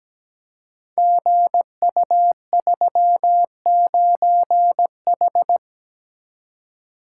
Morse code, heard as a keyed tone, about 17 wpm, 700 Hz. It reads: GU39H
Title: Code Groups - 5 Letter:Numbers - 17wpm